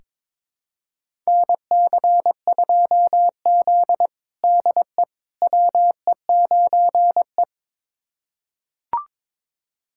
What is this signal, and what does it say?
Morse code, heard as a keyed tone, about 22 wpm, 700 Hz. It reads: NC2Z DE WE9E
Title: USA Callsign DE Callsign - 4 Characters - 22wpm